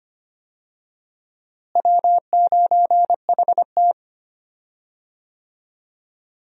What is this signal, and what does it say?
Morse code, heard as a keyed tone, about 25 wpm, 700 Hz. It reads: W9HT
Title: USA Call Signs - 4 Characters - Repeated 3x - 25wpm